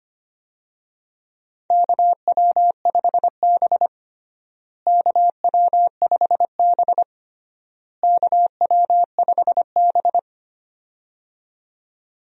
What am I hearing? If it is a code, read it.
Morse code, heard as a keyed tone, about 25 wpm, 700 Hz. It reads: KW5B KW5B KW5B
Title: USA Call Signs - 4 Characters - Repeated 3x - 25wpm